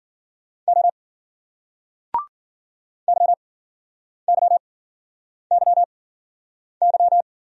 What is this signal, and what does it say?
Morse code, heard as a keyed tone, about 54 wpm, spaced as for 9 wpm, 700 Hz. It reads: O YYYY